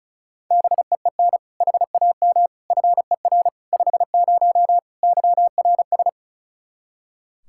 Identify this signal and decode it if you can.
Morse code, heard as a keyed tone, about 34 wpm, 700 Hz. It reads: BEEN HAM FER 50 YRS
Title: K1USN SST QSO Practice 35wpm